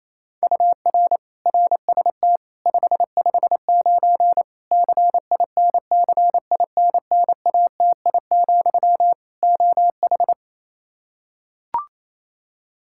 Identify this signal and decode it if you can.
Morse code, heard as a keyed tone, about 28 wpm, 700 Hz. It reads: UR RST 559 CINCINNATI, OH
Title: Practice QSO - Exchange 1 - Beginning - 28wpm